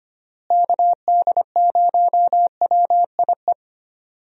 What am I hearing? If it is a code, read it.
Morse code, heard as a keyed tone, about 25 wpm, 700 Hz. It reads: KD0WIE
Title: USA Call Signs - 6 Characters - Repeated 2x - 25wpm